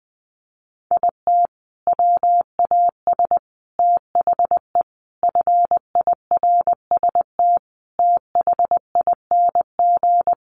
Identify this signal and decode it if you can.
Morse code, heard as a keyed tone, about 20 wpm, 700 Hz. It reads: IT WAS THE FIRST THING
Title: Sentences from Top 300 Words 20wpm